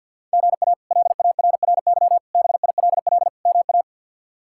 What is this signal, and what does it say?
Morse code, heard as a keyed tone, about 50 wpm, 700 Hz. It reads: GA LARRY BILL MA
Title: K1USN SST QSO Practice - 50wpm